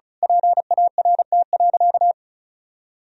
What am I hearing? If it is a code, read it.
Morse code, heard as a keyed tone, about 36 wpm, 700 Hz. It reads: PART.
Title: Sentences from Top 1000 Words - Review Part 1 - 35wpm